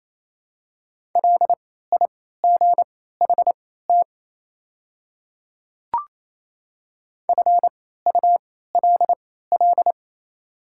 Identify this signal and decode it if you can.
Morse code, heard as a keyed tone, about 28 wpm, spaced as for 16 wpm, 700 Hz. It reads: LIGHT FULL
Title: 3x Character Spacing - Top 500 Words - 28wpm